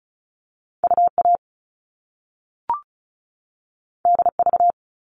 Morse code, heard as a keyed tone, about 36 wpm, 690 Hz. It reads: UA DV
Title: All Cut Numbers - 2-Digit 35wpm